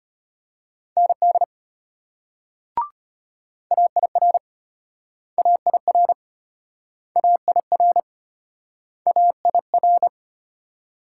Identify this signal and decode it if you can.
Morse code, heard as a keyed tone, about 33 wpm, 700 Hz. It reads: ND AIR AIR AIR AIR